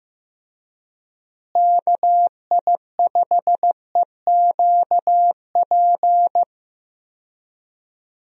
Morse code, heard as a keyed tone, about 15 wpm, 700 Hz. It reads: KI5EQP
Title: USA Call Signs - 6 Characters - Repeated 3x - 15wpm